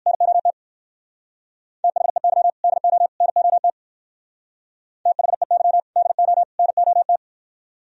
Morse code, heard as a keyed tone, about 60 wpm, 700 Hz. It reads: NOT THEY DO NOT THEY DO NOT